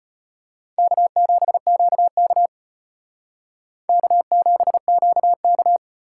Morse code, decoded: K7QK K7QK

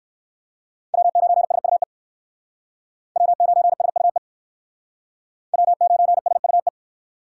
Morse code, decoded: W9SUE W9SUE W9SUE

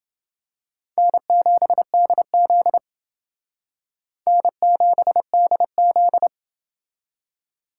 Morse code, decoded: N7DZ N7DZ

